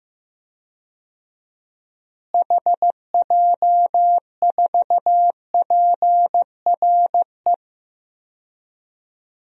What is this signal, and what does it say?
Morse code, heard as a keyed tone, about 15 wpm, 700 Hz. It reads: HJ4PRE
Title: DXCC-100 Call Signs - 6 Characters - Repeated 2x - 15wpm